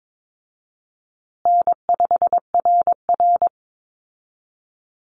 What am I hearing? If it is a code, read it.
Morse code, heard as a keyed tone, about 22 wpm, 700 Hz. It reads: N5RR
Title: Easy Call Signs Repeated 2x - 22wpm